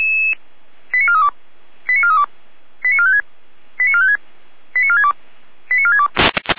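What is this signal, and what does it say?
Five-tone selective call (ZVEI), heard as quick runs of five tones, tones of about 70 ms. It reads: 89432 89432 89456 89456 89462 89462
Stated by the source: ZVEI five-tone selective call received on an SDR.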